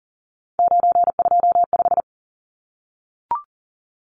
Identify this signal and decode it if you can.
Morse code, heard as a keyed tone, about 40 wpm, 700 Hz. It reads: 925
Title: Numbers - 3-Digit 40wpm